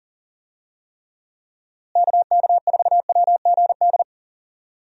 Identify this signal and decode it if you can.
Morse code, heard as a keyed tone, about 40 wpm, 700 Hz. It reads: KK4WGD
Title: USA Call Signs - 6 Characters - Repeated 3x - 40wpm